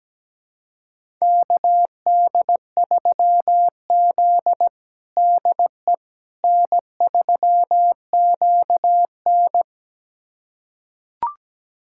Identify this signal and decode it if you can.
Morse code, heard as a keyed tone, about 17 wpm, 700 Hz. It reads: KD3Z DE N3QN
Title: USA Callsign DE Callsign - 4 Characters - 17wpm